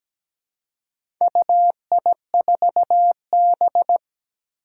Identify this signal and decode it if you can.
Morse code, heard as a keyed tone, about 17 wpm, 700 Hz. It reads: UI4B
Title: Four Challenging Letter-Number Combinations 17wpm